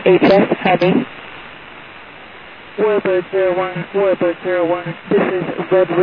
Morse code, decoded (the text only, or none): none